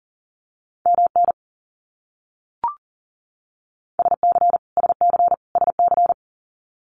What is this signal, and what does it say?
Morse code, heard as a keyed tone, about 40 wpm, 700 Hz. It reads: MN SC SC SC
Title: US States Repeated 3x 40wpm